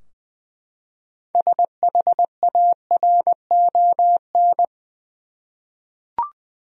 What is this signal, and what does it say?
Morse code, heard as a keyed tone, about 20 wpm, 700 Hz. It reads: SHARON
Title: USA Common Names - 20wpm